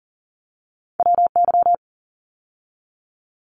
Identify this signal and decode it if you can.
Morse code, heard as a keyed tone, about 40 wpm, 700 Hz. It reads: WY
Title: ARRL Sections 40wpm